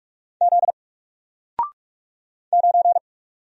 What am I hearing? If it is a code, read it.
Morse code, heard as a keyed tone, about 46 wpm, 700 Hz. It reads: Z 9
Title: Single Letter-Number 45wpm